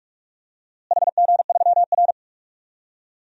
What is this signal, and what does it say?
Morse code, heard as a keyed tone, about 46 wpm, 700 Hz. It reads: HG3R
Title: DXCC-100 Call Signs - 4 Characters - Repeated 2x - 45wpm